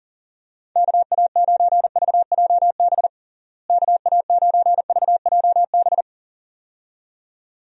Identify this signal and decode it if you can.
Morse code, heard as a keyed tone, about 40 wpm, 700 Hz. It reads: KA9VJB KA9VJB
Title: USA Call Signs - 6 Characters - Repeated 2x - 40wpm